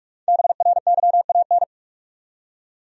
Morse code, heard as a keyed tone, about 46 wpm, 700 Hz. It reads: BRYAN
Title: USA Common Names - 45wpm